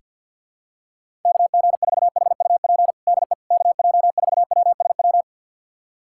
Morse code, heard as a keyed tone, about 48 wpm, 700 Hz. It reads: KG4HUP DE KJ4WSW
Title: USA Callsign DE Callsign - 6 Characters - 50wpm